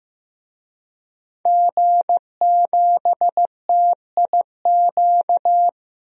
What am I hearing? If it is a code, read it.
Morse code, heard as a keyed tone, about 15 wpm, 700 Hz. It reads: G7TIQ
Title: DXCC-100 Call Signs - 5 Characters - Repeated 3x - 15wpm